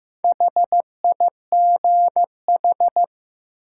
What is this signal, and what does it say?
Morse code, heard as a keyed tone, about 16 wpm, 700 Hz. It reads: HIGH
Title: Sentences from Top 200 Words - New Word Drill - 15wpm